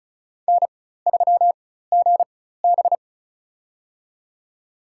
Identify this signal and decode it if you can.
Morse code, heard as a keyed tone, about 36 wpm, spaced as for 16 wpm, 700 Hz. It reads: N3GB
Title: USA Call Signs - 4 Characters - 4x Character Spacing - 35wpm